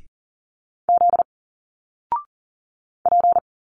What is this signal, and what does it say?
Morse code, heard as a keyed tone, about 40 wpm, 700 Hz. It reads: Z P
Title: Single Letters 40wpm